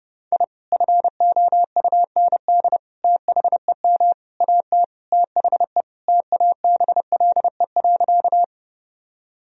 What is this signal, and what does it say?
Morse code, heard as a keyed tone, about 30 wpm, 700 Hz. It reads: I FOUND THEM AT THE TABLE.
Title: Sentences from Top 1000 Words - Review Part 2 - 30wpm